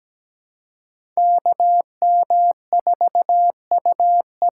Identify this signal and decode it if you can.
Morse code, heard as a keyed tone, about 17 wpm, 700 Hz. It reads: K M 4 U E
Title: USA Call Signs - 6 Characters - 6x Character Spacing - 17wpm